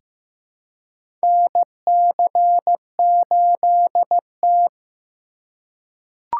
Morse code, heard as a keyed tone, about 15 wpm, 700 Hz. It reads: NC8T
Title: USA Call Signs - 4 Characters - 8x Character Spacing - 15wpm